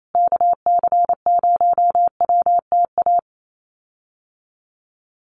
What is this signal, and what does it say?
Morse code, heard as a keyed tone, about 28 wpm, 700 Hz. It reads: KC0WTA
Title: USA Call Signs Repeated 3x 28wpm